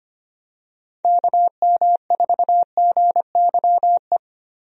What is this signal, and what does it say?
Morse code, heard as a keyed tone, about 25 wpm, 700 Hz. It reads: KM4GYE
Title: USA Call Signs - 6 Characters - Repeated 3x - 25wpm